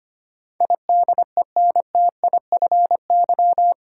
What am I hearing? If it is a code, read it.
Morse code, heard as a keyed tone, about 25 wpm, 700 Hz. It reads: IDENTIFY
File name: Eight Letter Words - 25wpm